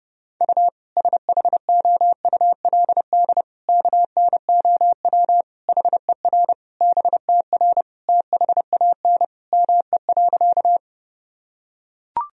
Morse code, heard as a keyed tone, about 32 wpm, 700 Hz. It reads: U SHOULD KNOW HER BTR THAN ME.
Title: Sentences from Top 400 Words and Abbreviations 30wpm